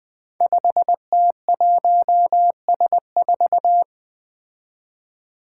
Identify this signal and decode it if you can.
Morse code, heard as a keyed tone, about 20 wpm, 700 Hz. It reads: 5T1S4
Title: Five Challenging Letter-Number Combinations 20wpm